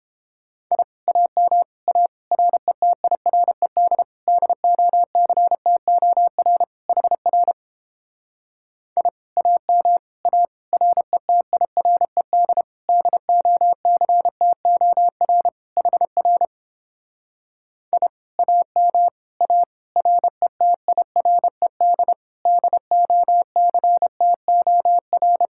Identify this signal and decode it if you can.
Morse code, heard as a keyed tone, about 31 wpm, 700 Hz. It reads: I AM A RETIRED DOCTOR HR I AM A RETIRED DOCTOR HR I AM A RETIRED DOCTOR